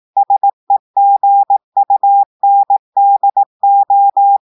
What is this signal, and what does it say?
Morse code, heard as a keyed tone, about 18 wpm, 800 Hz. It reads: SEGUNDO
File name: audios / portugais / portugais texte en clair 1 vitesse 18 mots minutes